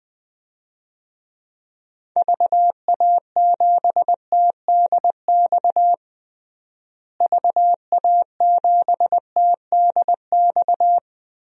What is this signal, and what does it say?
Morse code, heard as a keyed tone, about 20 wpm, 700 Hz. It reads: VA7TDX VA7TDX
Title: DXCC-100 Call Signs - 6 Characters - Repeated 2x - 20wpm